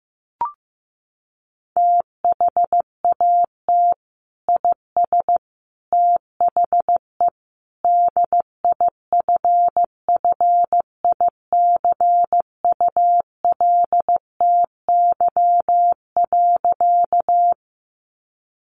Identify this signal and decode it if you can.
Morse code, heard as a keyed tone, about 15 wpm, 700 Hz. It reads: THAT IS THE DIFFICULTY.
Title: Sentences from Top 1500 Words 15wpm